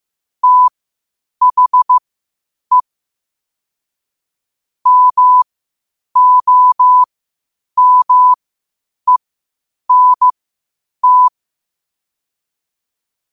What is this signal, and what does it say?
Morse code, heard as a keyed tone, about 15 wpm, spaced as for 8 wpm, 1000 Hz. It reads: THE MOMENT
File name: qotdfiles / 8wpm